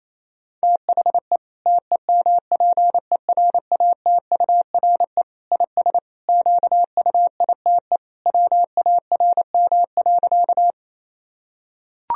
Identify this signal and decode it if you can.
Morse code, heard as a keyed tone, about 27 wpm, 700 Hz. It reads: THE TEMPERATURE IS QUITE WARM.
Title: Sentences from Top 1000 Words - Review Part 4 - 28wpm